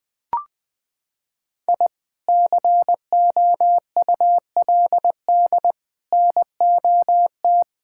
Morse code, heard as a keyed tone, about 20 wpm, 700 Hz. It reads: I COULD NOT
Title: Sentences with 3-Word Phrases - 20wpm